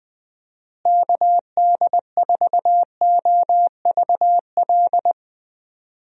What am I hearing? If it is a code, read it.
Morse code, heard as a keyed tone, about 20 wpm, 700 Hz. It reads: KD4OVL
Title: USA Call Signs - 6 Characters - Repeated 3x - 20wpm